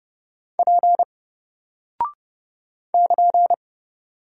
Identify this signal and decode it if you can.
Morse code, heard as a keyed tone, about 30 wpm, 700 Hz. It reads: P <KN>
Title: Single Character (Letter:Number:Punctuation:Prosign) 30wpm